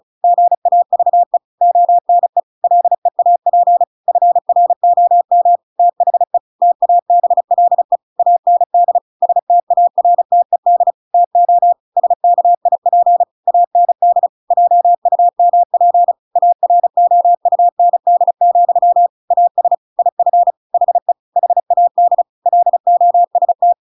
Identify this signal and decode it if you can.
Morse code, heard as a keyed tone, about 34 wpm, 700 Hz. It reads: GAVE ONE LEAP FROM THE TABLE AND STARTED TO SKIP AND JUMP AROUND, AS IF HE HAD LOST
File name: Pinocchio / Pinocchio ch8 35 35 ik1hge